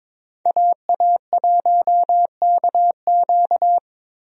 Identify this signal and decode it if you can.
Morse code, heard as a keyed tone, about 22 wpm, 700 Hz. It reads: AA1KQ
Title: USA Call Signs - 5 Characters - Repeated 3x - 22wpm